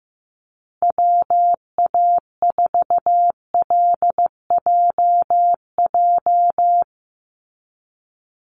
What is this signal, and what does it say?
Morse code, heard as a keyed tone, about 15 wpm, 700 Hz. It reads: WA4LJJ
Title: Call Signs Repeated 3x - 15wpm